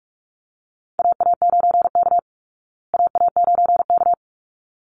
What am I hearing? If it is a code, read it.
Morse code, heard as a keyed tone, about 44 wpm, 700 Hz. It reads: AA9K AA9K
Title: Call Signs Repeated 3x - 45wpm